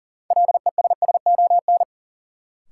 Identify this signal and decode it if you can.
Morse code, heard as a keyed tone, about 40 wpm, 700 Hz. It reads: LESSON
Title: Sentences from Top 1200 Words - New Word Drill Alt - 40wpm